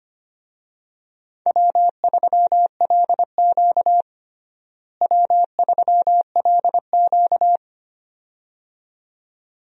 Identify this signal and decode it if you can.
Morse code, heard as a keyed tone, about 25 wpm, 700 Hz. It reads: W3LQ W3LQ
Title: USA Call Signs - 4 Characters - Repeated 2x - 25wpm